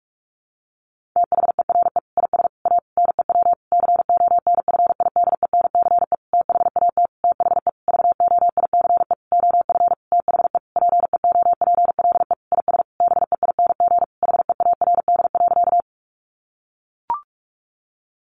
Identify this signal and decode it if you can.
Morse code, heard as a keyed tone, about 44 wpm, 700 Hz. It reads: THERE IS A NEW CONFIDENCE THAT THE VOICE OF THE PEOPLE IS BEING HEARD.
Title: Sentences from Top 1500 Words 45wpm